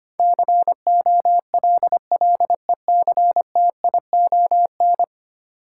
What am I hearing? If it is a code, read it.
Morse code, heard as a keyed tone, about 25 wpm, 700 Hz. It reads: COLLECTION
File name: Ten Letter Words - 25wpm